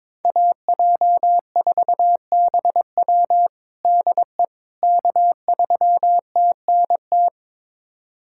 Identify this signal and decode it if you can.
Morse code, heard as a keyed tone, about 22 wpm, 700 Hz. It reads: AJ4BW DE K3TNT
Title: USA Callsign DE Callsign - 5 Characters - 22wpm